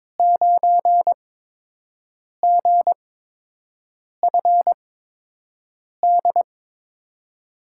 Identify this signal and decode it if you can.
Morse code, heard as a keyed tone, about 22 wpm, spaced as for 6 wpm, 700 Hz. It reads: 9GFD
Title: USA Call Signs - 5 Characters - 8x Character Spacing - 22wpm